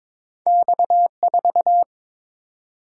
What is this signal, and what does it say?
Morse code, heard as a keyed tone, about 22 wpm, 700 Hz. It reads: X4
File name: practice / 22wpm / Two Letter-Number Combinations - 22wpm